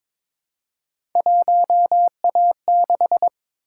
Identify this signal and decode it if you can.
Morse code, heard as a keyed tone, about 22 wpm, 700 Hz. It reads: 1A6
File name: practice / 22wpm / Three Letter-Number Combinations - 22wpm